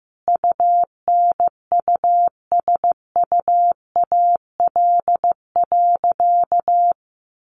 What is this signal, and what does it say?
Morse code, heard as a keyed tone, about 15 wpm, 700 Hz. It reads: UNUSUAL.
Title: Sentences from Top 1400 Words 15wpm